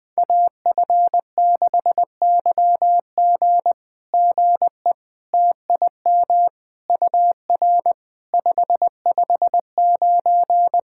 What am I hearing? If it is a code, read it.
Morse code, heard as a keyed tone, about 21 wpm, 700 Hz. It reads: AF6YG GE TIM UR 559
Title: SOTA QSO Practice 20wpm